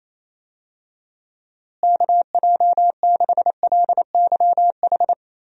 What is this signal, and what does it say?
Morse code, heard as a keyed tone, about 28 wpm, 700 Hz. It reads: KJ6LYH
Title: USA Call Signs - 6 Characters - Repeated 2x - 28wpm